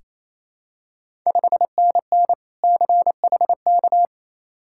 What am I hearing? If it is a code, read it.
Morse code, heard as a keyed tone, about 28 wpm, 700 Hz. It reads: 5NN CHK
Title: Ontario QSO Party Exchange 28wpm